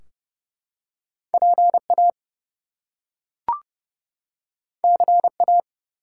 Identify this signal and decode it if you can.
Morse code, heard as a keyed tone, about 30 wpm, 700 Hz. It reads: PA CA
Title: US States and Canadian Provinces 30wpm